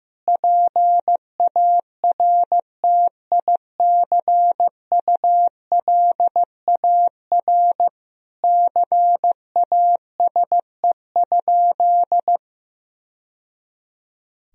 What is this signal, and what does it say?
Morse code, heard as a keyed tone, about 15 wpm, 700 Hz. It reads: PARTICULAR CASE?
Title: Sentences from Top 1000 Words - Review Part 4 - 15wpm